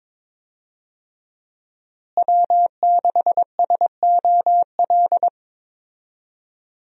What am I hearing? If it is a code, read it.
Morse code, heard as a keyed tone, about 22 wpm, 700 Hz. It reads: W6SOL
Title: USA Call Signs - 5 Characters - Repeated 3x - 22wpm